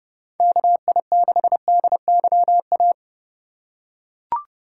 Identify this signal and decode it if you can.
Morse code, heard as a keyed tone, about 30 wpm, 700 Hz. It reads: KI6DYA
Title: USA Call Signs - 6 Characters - 2x Character Spacing - 30wpm